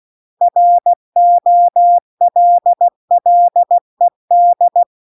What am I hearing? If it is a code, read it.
Morse code, heard as a keyed tone, about 16 wpm, 700 Hz. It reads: ROLLED